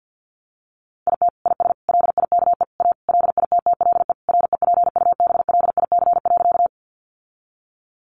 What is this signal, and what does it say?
Morse code, heard as a keyed tone, about 50 wpm, 700 Hz. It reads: IT IS LIKE A LITTLE REPUBLIC.
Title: Sentences from Top 1500 Words 50wpm